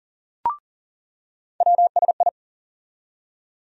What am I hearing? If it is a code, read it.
Morse code, heard as a keyed tone, about 40 wpm, 700 Hz. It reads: WSI
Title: USA Call Sign Postfixes - 3 Letters 40wpm